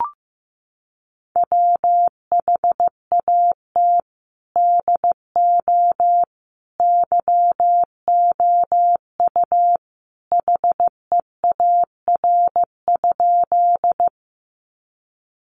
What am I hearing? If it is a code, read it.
Morse code, heard as a keyed tone, about 15 wpm, 700 Hz. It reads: WHAT DO YOU HEAR?
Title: Sentences from Top 300 Words 15wpm